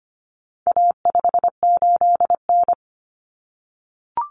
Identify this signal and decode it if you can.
Morse code, heard as a keyed tone, about 25 wpm, 700 Hz. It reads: A58N
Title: Common Cut Numbers - 4-Digit 25wpm